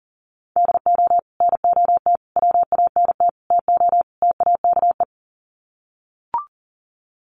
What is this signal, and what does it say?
Morse code, heard as a keyed tone, about 40 wpm, 700 Hz. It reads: DO NOT WANT TO TAKE
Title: Sets of 5 Words Encore2 40wpm